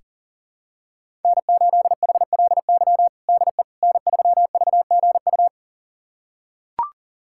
Morse code, heard as a keyed tone, about 40 wpm, 700 Hz. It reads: N8HLY DE N3VGU
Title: USA Callsign DE Callsign - 5 Characters - 40wpm